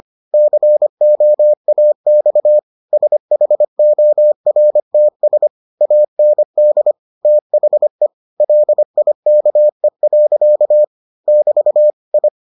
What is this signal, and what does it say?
Morse code, heard as a keyed tone, about 25 wpm, 600 Hz. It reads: COAX SHORTS AND THE LIKE. <BT> I